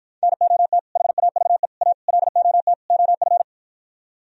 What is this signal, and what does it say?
Morse code, heard as a keyed tone, about 52 wpm, 700 Hz. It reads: NOT HAVE A LOT OF